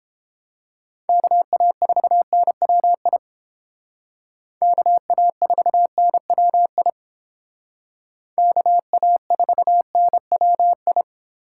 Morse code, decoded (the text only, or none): KA4NWI KA4NWI KA4NWI